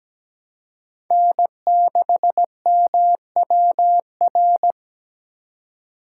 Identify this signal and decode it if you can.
Morse code, heard as a keyed tone, about 17 wpm, 700 Hz. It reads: N6MWR
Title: USA Call Signs - 5 Characters - 2x Character Spacing - 17wpm